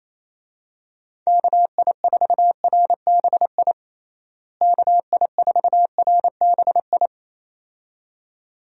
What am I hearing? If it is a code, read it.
Morse code, heard as a keyed tone, about 28 wpm, 700 Hz. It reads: KI4RBI KI4RBI
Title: USA Call Signs - 6 Characters - Repeated 2x - 28wpm